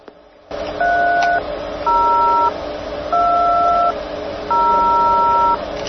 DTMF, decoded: 3020